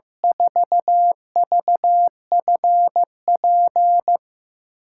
Morse code, heard as a keyed tone, about 15 wpm, 700 Hz. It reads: 4VFP
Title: USA Call Signs - 6 Characters - 6x Character Spacing - 15wpm